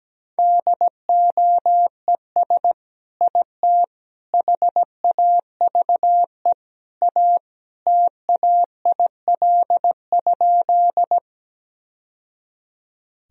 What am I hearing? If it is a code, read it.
Morse code, heard as a keyed tone, about 17 wpm, 700 Hz. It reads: DOES IT HAVE A TAIL?
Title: Sentences from Top 1000 Words - Review Part 3 - 17wpm